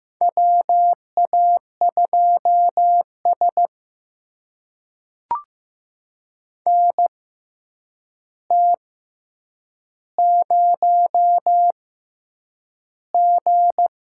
Morse code, heard as a keyed tone, about 15 wpm, spaced as for 8 wpm, 700 Hz. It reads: WA2S NT0G
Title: USA Call Signs - 6x Character Spacing - 15wpm